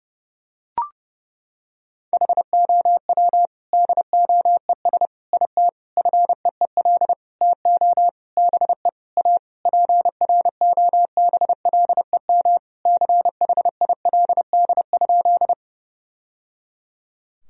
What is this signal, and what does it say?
Morse code, heard as a keyed tone, about 29 wpm, 700 Hz. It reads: HOW DOES IT FEEL TO BE A PROBLEM CHILD?
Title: Sentences from Top 1000 Words - Review Part 2 - 30wpm